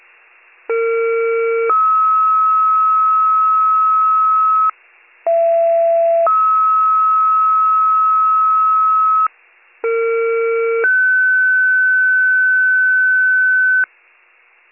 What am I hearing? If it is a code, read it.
Two-tone page, heard as a one-second tone, then a three-second tone, about 460 then 1290 Hz, about 670 then 1290 Hz, about 460 then 1500 Hz, with noise. It is heard from 0.7 to 13.8 s.